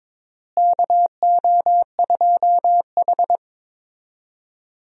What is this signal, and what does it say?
Morse code, heard as a keyed tone, about 22 wpm, 700 Hz. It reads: KO2H
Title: USA Call Signs - 4 Characters - Repeated 3x - 22wpm